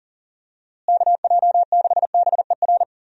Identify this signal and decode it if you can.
Morse code, heard as a keyed tone, about 40 wpm, 700 Hz. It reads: KJ6BER
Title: USA Call Signs - 6 Characters - Repeated 3x - 40wpm